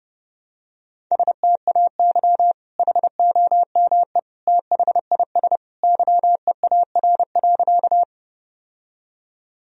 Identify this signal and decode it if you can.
Morse code, heard as a keyed tone, about 30 wpm, 700 Hz. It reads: STAY HOME THIS YEAR.
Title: Sentences from Top 1000 Words - Review Part 1 - 30wpm